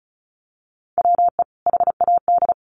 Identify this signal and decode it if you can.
Morse code, heard as a keyed tone, about 34 wpm, 700 Hz. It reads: WE HAD
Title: Sentences from Top 1400 Words 35wpm